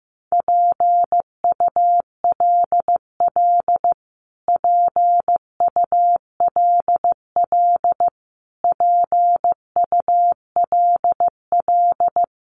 Morse code, heard as a keyed tone, about 15 wpm, 700 Hz. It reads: PULL PULL PULL
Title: Top 500 Words Repeated 15wpm